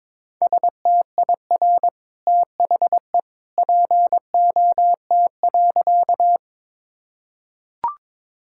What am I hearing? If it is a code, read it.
Morse code, heard as a keyed tone, about 22 wpm, 700 Hz. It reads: STIR THE POT.
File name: practice / 22wpm / Easy Sentences with Hints - 22wpm